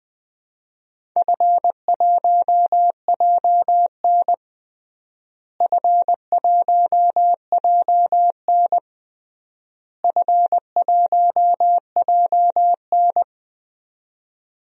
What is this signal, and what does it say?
Morse code, heard as a keyed tone, about 21 wpm, 700 Hz. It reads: F1JN F1JN F1JN
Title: DXCC-100 Call Signs - 4 Characters - Repeated 3x - 20wpm